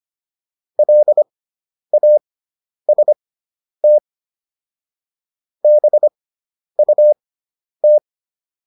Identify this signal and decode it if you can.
Morse code, heard as a keyed tone, about 25 wpm, spaced as for 10 wpm, 600 Hz. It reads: LAST BUT